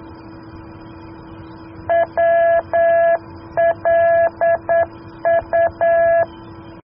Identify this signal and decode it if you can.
Morse code, heard as a keyed tone, about 9 wpm, 670 Hz. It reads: WLU